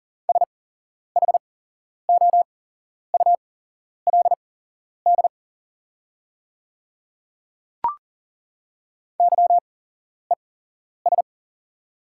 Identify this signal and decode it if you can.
Morse code, heard as a keyed tone, about 39 wpm, spaced as for 11 wpm, 700 Hz. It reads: SHOULD YES
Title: 8x Character Spacing - Top 500 Words - 40wpm